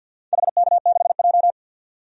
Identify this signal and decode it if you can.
Morse code, heard as a keyed tone, about 51 wpm, 700 Hz. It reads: HK6J
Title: DXCC-100 Call Signs - 4 Characters - 50wpm